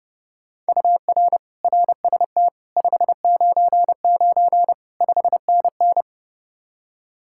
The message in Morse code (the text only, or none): UR RST 599 5NN